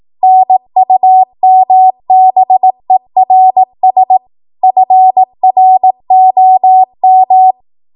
Morse code, heard as a keyed tone, about 18 wpm, 750 Hz. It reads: NUMBERS FROM
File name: Archive / 18 WPM / 18WPM